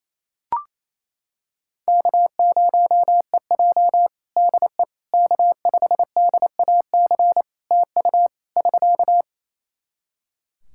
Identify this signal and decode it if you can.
Morse code, heard as a keyed tone, about 28 wpm, 700 Hz. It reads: K0EJ DE K5DAC TU <SK>
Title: Practice QSO - Exchange 4 - Ending - 28wpm